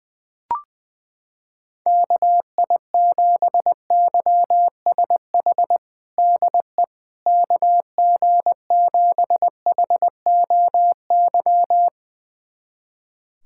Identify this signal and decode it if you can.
Morse code, heard as a keyed tone, about 20 wpm, 700 Hz. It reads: KI7YSH DE KG7HOY
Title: USA Callsign DE Callsign 20wpm